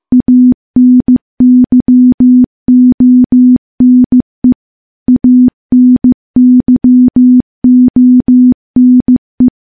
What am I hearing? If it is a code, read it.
Morse code, heard as a keyed tone, about 15 wpm, 250 Hz. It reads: ANYONE ANYONE
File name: morse-training-15WPM